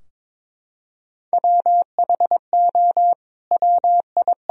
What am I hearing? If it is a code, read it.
Morse code, heard as a keyed tone, about 22 wpm, 700 Hz. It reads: WHO WI
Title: Sentences from Top 1000 Words - Review Part 3 - 22wpm